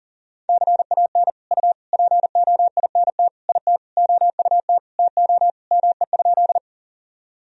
Morse code, heard as a keyed tone, about 40 wpm, 700 Hz. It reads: CAN U POINT IT OUT TO ME?
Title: Sentences from Top 200 Words and Abbreviations 40wpm